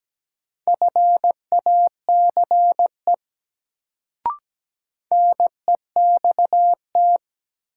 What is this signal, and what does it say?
Morse code, heard as a keyed tone, about 17 wpm, 700 Hz. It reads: FACE NEXT
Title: Sentences from Top 300 Words - New Word Drill - 17wpm